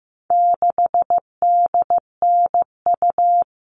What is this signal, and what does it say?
Morse code, heard as a keyed tone, about 15 wpm, 700 Hz. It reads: 6DNU
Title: All Cut Numbers - 4-Digit 15wpm